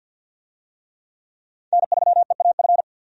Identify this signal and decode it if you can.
Morse code, heard as a keyed tone, about 47 wpm, 700 Hz. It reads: N3EAF
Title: USA Call Signs - 5 Characters - 50wpm